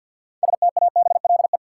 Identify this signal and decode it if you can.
Morse code, heard as a keyed tone, about 53 wpm, 700 Hz. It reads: STABLE
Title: Sentences from Top 1500 Words - New Word Drill - 50wpm